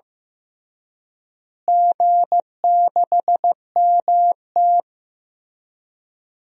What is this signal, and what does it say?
Morse code, heard as a keyed tone, about 16 wpm, 700 Hz. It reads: G6MT
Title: DXCC-100 Call Signs - 4 Characters - Repeated 3x - 15wpm